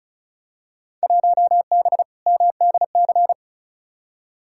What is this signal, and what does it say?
Morse code, heard as a keyed tone, about 34 wpm, 700 Hz. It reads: 1B MDC
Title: Field Day Exchanges Encore - 35wpm